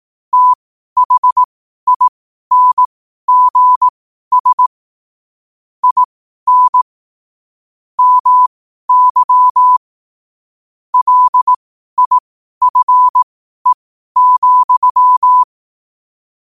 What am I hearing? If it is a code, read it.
Morse code, heard as a keyed tone, about 18 wpm, spaced as for 12 wpm, 1000 Hz. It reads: THINGS IN MY LIFE,
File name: qotdfiles / 12wpm